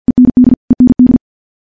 Morse code, heard as a keyed tone, about 50 wpm, 250 Hz. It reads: END END
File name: morse / morse-training-50WPM